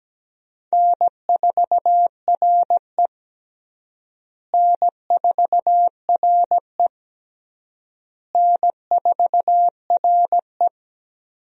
Morse code, heard as a keyed tone, about 17 wpm, 700 Hz. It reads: N4RE N4RE N4RE
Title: USA Call Signs - 4 Characters - Repeated 3x - 17wpm